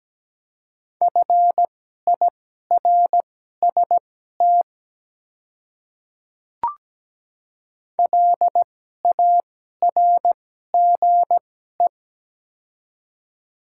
Morse code, heard as a keyed tone, about 17 wpm, spaced as for 12 wpm, 700 Hz. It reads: FIRST LARGE
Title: 2x Character Spacing - Top 200 Words - 17wpm